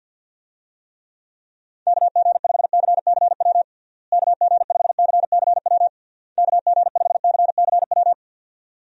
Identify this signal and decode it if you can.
Morse code, heard as a keyed tone, about 52 wpm, 700 Hz. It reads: KG5CCW KG5CCW KG5CCW
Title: USA Call Signs - 6 Characters - Repeated 3x - 50wpm